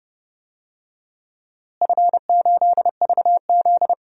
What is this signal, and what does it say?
Morse code, heard as a keyed tone, about 30 wpm, 700 Hz. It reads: F8VZ
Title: DXCC-100 Call Signs - 4 Characters - Repeated 3x - 30wpm